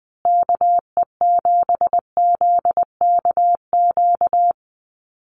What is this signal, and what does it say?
Morse code, heard as a keyed tone, about 19 wpm, 700 Hz. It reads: KE7ZKQ
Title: USA Call Signs Repeated 2x 20wpm